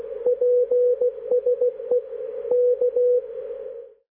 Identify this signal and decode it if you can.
Morse code, heard as a keyed tone, about 16 wpm, 490 Hz, with radio noise. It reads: PSE K
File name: PA0A- -1856z-502.5kHz1